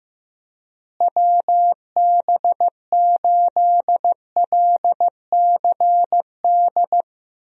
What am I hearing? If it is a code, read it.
Morse code, heard as a keyed tone, about 15 wpm, 700 Hz. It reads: WB8LCD
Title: USA Call Signs - 6 Characters - Repeated 2x - 15wpm